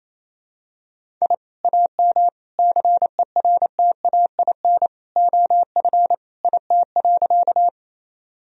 Morse code, heard as a keyed tone, about 27 wpm, 700 Hz. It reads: I AM CERTAIN OF IT.
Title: Sentences from Top 1000 Words - Review Part 2 - 28wpm